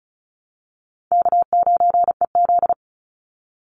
Morse code, heard as a keyed tone, about 34 wpm, 700 Hz. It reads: K9EZ
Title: NA SOTA Chaser Callsigns 35wpm